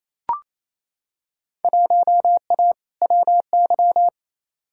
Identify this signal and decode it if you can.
Morse code, heard as a keyed tone, about 28 wpm, 700 Hz. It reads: 1A WY
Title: Field Day Exchanges Encore - 28wpm